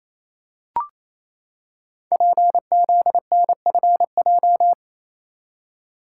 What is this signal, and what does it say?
Morse code, heard as a keyed tone, about 28 wpm, 700 Hz. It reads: PZNFJ
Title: Code Groups - 5 Letters - 28wpm